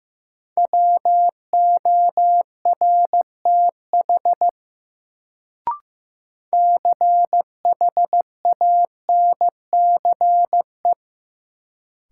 Morse code, heard as a keyed tone, about 15 wpm, 700 Hz. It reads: WORTH CHANCE
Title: Sentences from Top 800 Words - New Word Drill - 15wpm